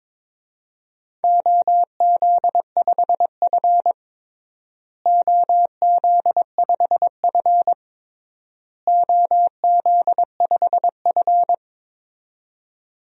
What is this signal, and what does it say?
Morse code, heard as a keyed tone, about 22 wpm, 700 Hz. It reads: OZ5F OZ5F OZ5F
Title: DXCC-100 Call Signs - 4 Characters - Repeated 3x - 22wpm